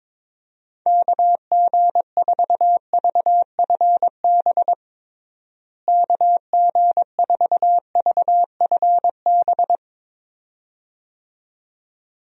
Morse code, heard as a keyed tone, about 22 wpm, 700 Hz. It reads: KG4VFB KG4VFB
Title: USA Call Signs - 6 Characters - Repeated 2x - 22wpm